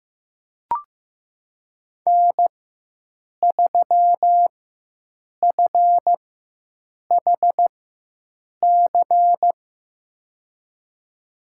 Morse code, heard as a keyed tone, about 15 wpm, spaced as for 7 wpm, 700 Hz. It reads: N3FHC
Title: USA Call Signs - 5 Characters - 4x Character Spacing - 15wpm